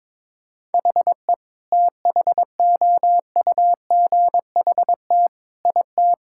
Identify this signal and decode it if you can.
Morse code, heard as a keyed tone, about 22 wpm, 700 Hz. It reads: HE THOUGHT IT
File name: practice / 22wpm / Sentences with 8-Word Phrases - 22wpm